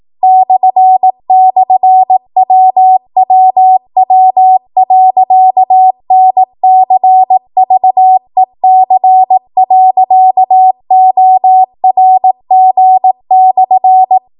Morse code, heard as a keyed tone, about 18 wpm, 750 Hz. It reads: //WWW.NCVEC.ORG/